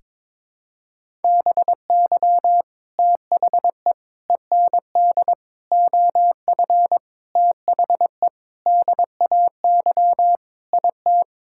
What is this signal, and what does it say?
Morse code, heard as a keyed tone, about 21 wpm, 700 Hz. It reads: BY THE END OF THE DAY IT
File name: practice / 22wpm / Sentences with 4-Word Phrases - 22wpm